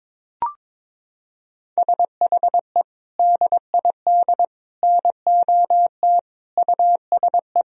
Morse code, heard as a keyed tone, about 22 wpm, 700 Hz. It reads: SHE DID NOT USE
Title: Sentences from Top 1000 Words - Review Part 1 - 22wpm